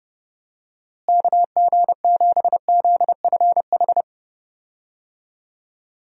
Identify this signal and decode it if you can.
Morse code, heard as a keyed tone, about 30 wpm, 700 Hz. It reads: KG7ZFH
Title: USA Call Signs - 6 Characters - Repeated 2x - 30wpm